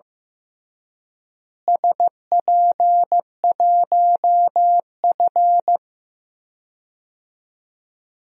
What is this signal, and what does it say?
Morse code, heard as a keyed tone, about 15 wpm, 700 Hz. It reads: SP1F
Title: DXCC-100 Call Signs - 4 Characters - Repeated 2x - 15wpm